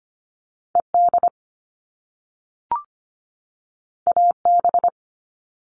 Morse code, heard as a keyed tone, about 25 wpm, 700 Hz. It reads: ED AB
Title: All Cut Numbers - 2-Digit 25wpm